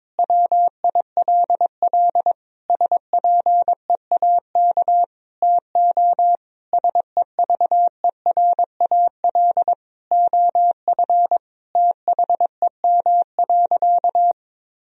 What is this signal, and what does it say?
Morse code, heard as a keyed tone, about 22 wpm, 700 Hz. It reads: WILL SPEAK TO SEVERAL OF THEM.
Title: Sentences from Top 1000 Words - Review Part 2 - 22wpm